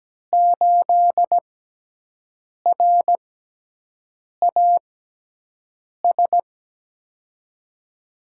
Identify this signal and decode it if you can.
Morse code, heard as a keyed tone, about 17 wpm, spaced as for 6 wpm, 700 Hz. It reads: 8RAS
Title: USA Call Signs - 5 Characters - 6x Character Spacing - 17wpm